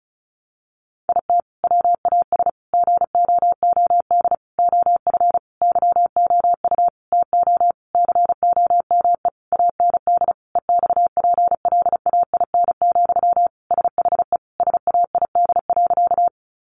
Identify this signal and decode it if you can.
Morse code, heard as a keyed tone, about 34 wpm, 700 Hz. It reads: IT WAS GOOD OF YOU TO COME AND EXPLAIN, SHE SAID.
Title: Sentences from Top 1100 Words 35wpm